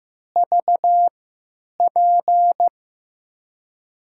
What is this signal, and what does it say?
Morse code, heard as a keyed tone, about 15 wpm, spaced as for 9 wpm, 700 Hz. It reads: VP
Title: USA Call Signs - 6 Characters - 3x Character Spacing - 15wpm